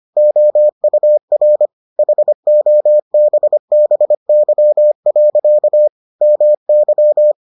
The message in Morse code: OUR HOBBY. MY

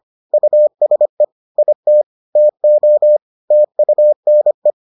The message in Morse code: USE IT TO TUNE

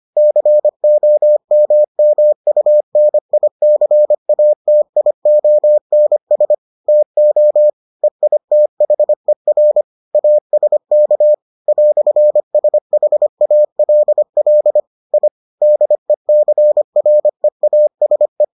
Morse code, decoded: COMMUNICATIONS TO EITHER ASK "SHALL I DECREASE

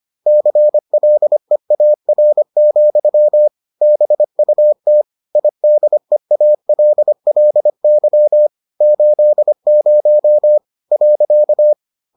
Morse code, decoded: CLEAR, BUT IDEALLY 80.

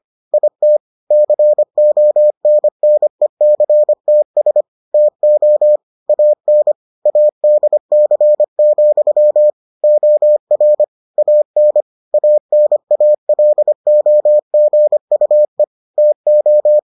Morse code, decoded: IT CONNECTS TO AN ADC, OR AN ANALOGUE TO

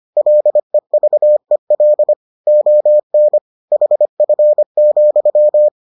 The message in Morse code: LEVEL ON HF,

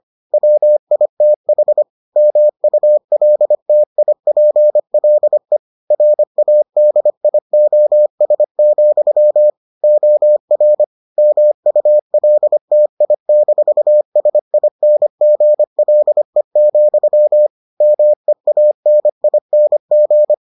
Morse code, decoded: WITH MULTIPLE RADIOS, OR MULTI-SINGLE, MEANING

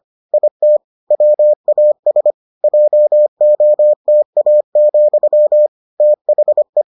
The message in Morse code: IT WAS JOTA, THE

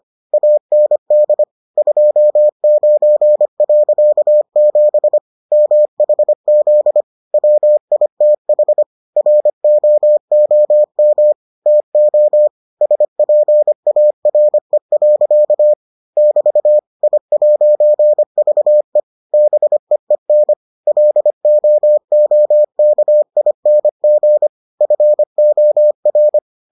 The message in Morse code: AND 29.7 MHZ WITH ROOM TO SPARE. <BT> I'VE BEEN LOOKING FOR

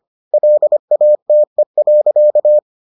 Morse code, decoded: LATE.

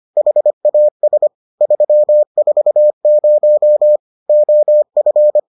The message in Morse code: HAS 340 OF

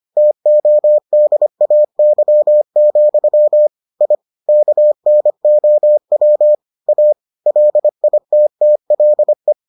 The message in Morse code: TODAY, I KNOW A LITTLE

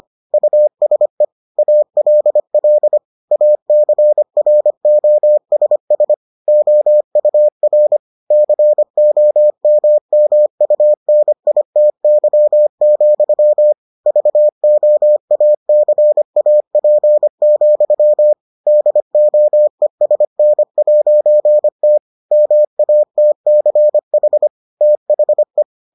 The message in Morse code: USE ALL ACROSS OUR COMMUNITY, VOACAP, DOESN'T MATCH THE

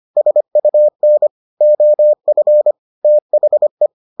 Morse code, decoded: SUN OF THE

